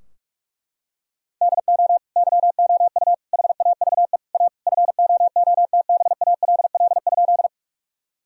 Morse code, heard as a keyed tone, about 41 wpm, 700 Hz. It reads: DO YOU HAVE A FOOTBALL?
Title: Sentences from Top 1600 Words 45wpm